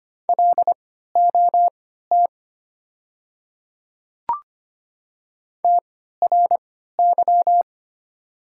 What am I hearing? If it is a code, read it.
Morse code, heard as a keyed tone, about 25 wpm, spaced as for 14 wpm, 700 Hz. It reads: LOT TRY
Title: 3x Character Spacing - Top 500 Words - 25wpm